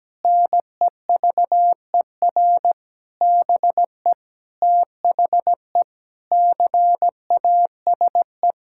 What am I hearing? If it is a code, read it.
Morse code, heard as a keyed tone, about 17 wpm, 700 Hz. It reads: NEVER BE THE CASE
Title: Sentences from Top 1600 Words 17wpm